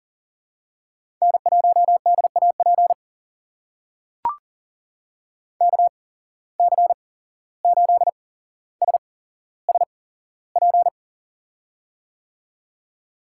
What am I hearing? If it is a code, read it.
Morse code, heard as a keyed tone, about 40 wpm, spaced as for 16 wpm, 700 Hz. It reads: N1DAP KC8SSP
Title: USA Call Signs - 8x Character Spacing - 40wpm